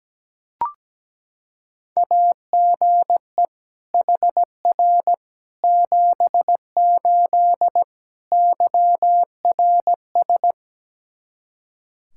Morse code, heard as a keyed tone, about 17 wpm, 700 Hz. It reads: AGE HR 78 YRS
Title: Practice QSO - Exchange 3 - More Info - 17wpm